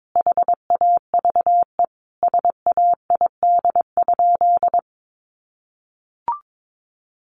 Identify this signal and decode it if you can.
Morse code, heard as a keyed tone, about 22 wpm, 700 Hz. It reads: HAVE SAID?
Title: Sentences from Top 1500 Words 22wpm